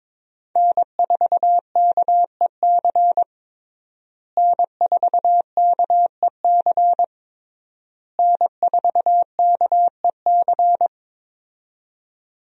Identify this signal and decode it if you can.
Morse code, heard as a keyed tone, about 22 wpm, 700 Hz. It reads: N4KEC N4KEC N4KEC
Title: USA Call Signs - 5 Characters - Repeated 3x - 22wpm